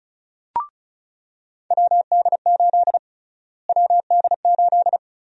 Morse code, heard as a keyed tone, about 34 wpm, 700 Hz. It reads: WD8 WD8
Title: USA Call Sign Prefixes - Repeated 2x - 35wpm